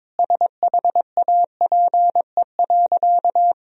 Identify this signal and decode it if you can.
Morse code, heard as a keyed tone, about 22 wpm, 700 Hz. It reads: SHAPE.
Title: Sentences from Top 1000 Words - Review Part 3 - 22wpm